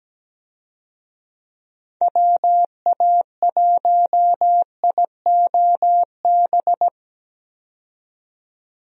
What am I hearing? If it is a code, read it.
Morse code, heard as a keyed tone, about 17 wpm, 700 Hz. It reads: WA1IOB
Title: USA Call Signs - 6 Characters - 17wpm